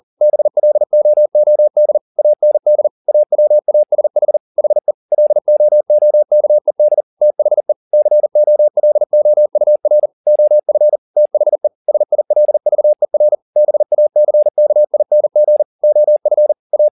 Morse code, heard as a keyed tone, about 40 wpm, 600 Hz. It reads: BLOOD AND AWASH HE LOOKED THE COLOUR OF THE SILVER BACKING OF A
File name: the old man and the sea 2 40wpm